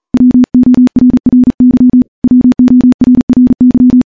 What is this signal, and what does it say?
Morse code, heard as a keyed tone, about 40 wpm, 250 Hz. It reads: WORRY WORRY
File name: morse / morse-training-40WPM